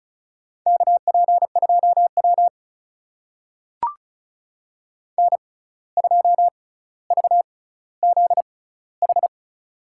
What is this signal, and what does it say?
Morse code, heard as a keyed tone, about 36 wpm, spaced as for 17 wpm, 700 Hz. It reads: KP2W N2VZH
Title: USA Call Signs - 6x Character Spacing - 35wpm